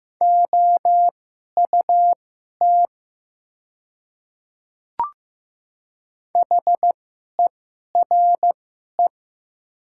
Morse code, heard as a keyed tone, about 15 wpm, spaced as for 11 wpm, 700 Hz. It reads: OUT HERE
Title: 2x Character Spacing - Top 200 Words - 15wpm